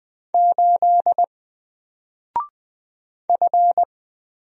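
Morse code, heard as a keyed tone, about 20 wpm, 700 Hz. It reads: 8 F
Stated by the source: Single letter and number Morse drill at 20 wpm, from N4LCD.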